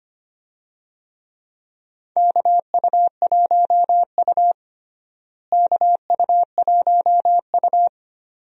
Morse code, heard as a keyed tone, about 25 wpm, 700 Hz. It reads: KU1U KU1U
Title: USA Call Signs - 4 Characters - Repeated 2x - 25wpm